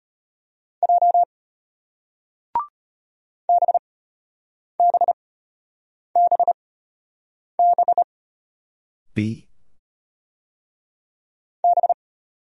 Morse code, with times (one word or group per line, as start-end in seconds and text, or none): 0.8-1.2 J
3.5-8.0 BBBB
11.6-11.9 B